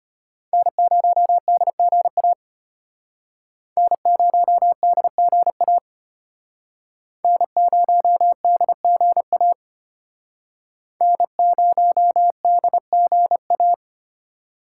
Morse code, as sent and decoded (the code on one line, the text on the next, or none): -. ----- -.. --. .- / -. ----- -.. --. .- / -. ----- -.. --. .- / -. ----- -.. --. .-
N0DGA N0DGA N0DGA N0DGA